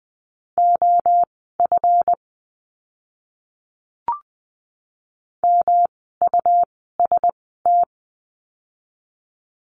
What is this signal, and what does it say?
Morse code, heard as a keyed tone, about 20 wpm, spaced as for 15 wpm, 700 Hz. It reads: OF MUST
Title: 237 Lesson 38f - Letter q - Words 20wpm